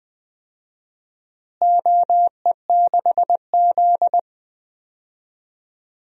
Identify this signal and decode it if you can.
Morse code, heard as a keyed tone, about 20 wpm, 700 Hz. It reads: OE6Z
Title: DXCC-100 Call Signs - 4 Characters - Repeated 3x - 20wpm